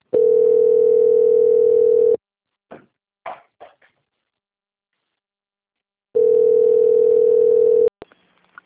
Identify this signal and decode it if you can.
Ringback tone (North America), 440 + 480 Hz.